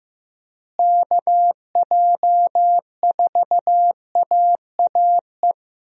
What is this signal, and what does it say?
Morse code, heard as a keyed tone, about 15 wpm, 700 Hz. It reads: KJ4AAE
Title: USA Call Signs - 6 Characters - Repeated 2x - 15wpm